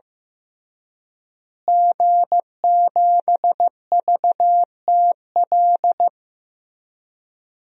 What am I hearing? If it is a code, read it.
Morse code, heard as a keyed tone, about 15 wpm, 700 Hz. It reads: G7VTL
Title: DXCC-100 Call Signs - 5 Characters - Repeated 3x - 15wpm